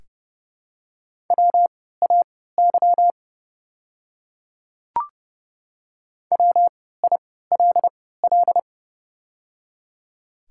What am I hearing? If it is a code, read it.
Morse code, heard as a keyed tone, about 30 wpm, spaced as for 17 wpm, 700 Hz. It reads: WAY WILL
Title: 3x Character Spacing - Top 100 Words - 30wpm